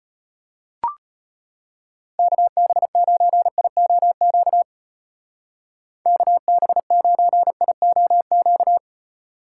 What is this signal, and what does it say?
Morse code, heard as a keyed tone, about 32 wpm, 700 Hz. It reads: KB9IOQ KB9IOQ